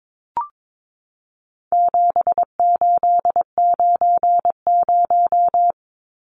Morse code, heard as a keyed tone, about 22 wpm, 700 Hz. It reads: 7890
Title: Numbers - 4-Digit 22wpm